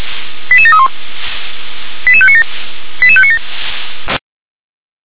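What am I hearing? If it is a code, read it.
Five-tone selective call (ZVEI), heard as quick runs of five tones, tones of about 70 ms. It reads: 88731 88597 88597
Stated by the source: ZVEI five-tone call 88510 received on SDR.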